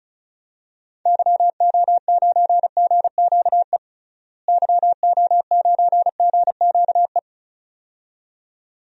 Morse code, heard as a keyed tone, about 34 wpm, 700 Hz. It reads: YO9GQE YO9GQE
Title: DXCC-100 Call Signs - 6 Characters - Repeated 2x - 35wpm